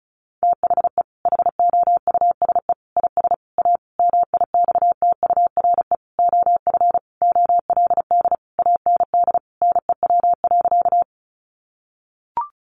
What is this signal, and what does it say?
Morse code, heard as a keyed tone, about 34 wpm, 700 Hz. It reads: THE HOUSE IS A MIXTURE OF OLD AND NEW.
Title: Sentences from Top 1500 Words 35wpm